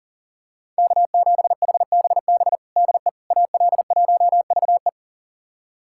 Morse code, heard as a keyed tone, about 40 wpm, 700 Hz. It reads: K7HBB DE AL1VE
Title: USA Callsign DE Callsign - 5 Characters - 40wpm